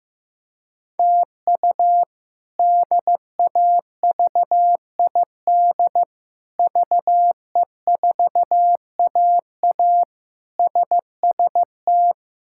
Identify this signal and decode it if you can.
Morse code, heard as a keyed tone, about 16 wpm, 700 Hz. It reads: TU DAVID VE4AA SST
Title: K1USN SST QSO Practice - 15wpm